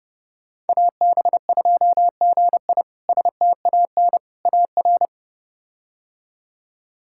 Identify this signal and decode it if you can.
Morse code, heard as a keyed tone, about 32 wpm, 700 Hz. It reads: AB2GI STAN AR
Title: K1USN SST QSO Practice - 30wpm